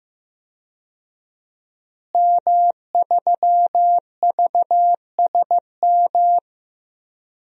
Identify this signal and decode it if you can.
Morse code, heard as a keyed tone, about 15 wpm, 700 Hz. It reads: M3VSM
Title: DXCC-100 Call Signs - 5 Characters - Repeated 3x - 15wpm